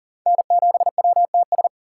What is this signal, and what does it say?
Morse code, heard as a keyed tone, about 37 wpm, 700 Hz. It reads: N7WTS
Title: USA Call Signs - 5 Characters - 3x Character Spacing - 40wpm